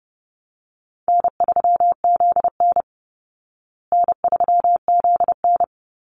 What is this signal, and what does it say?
Morse code, heard as a keyed tone, about 30 wpm, 700 Hz. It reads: N3ZN N3ZN
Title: Call Signs Repeated 2x - 30wpm